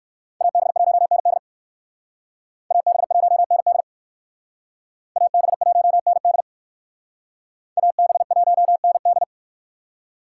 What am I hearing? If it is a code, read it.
Morse code, heard as a keyed tone, about 58 wpm, 700 Hz. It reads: AB1ND AB1ND AB1ND AB1ND